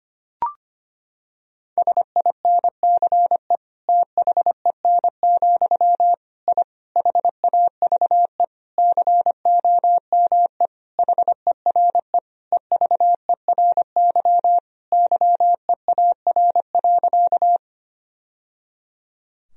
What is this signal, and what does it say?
Morse code, heard as a keyed tone, about 25 wpm, 700 Hz. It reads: SINCE THEN, I HAVE COME HERE EVERY YEAR.
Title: Sentences from Top 1000 Words - Review Part 1 - 25wpm